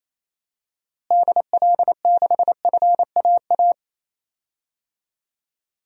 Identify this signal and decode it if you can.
Morse code, heard as a keyed tone, about 28 wpm, 700 Hz. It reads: DL6FAA
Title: DXCC-100 Call Signs - 6 Characters - Repeated 3x - 28wpm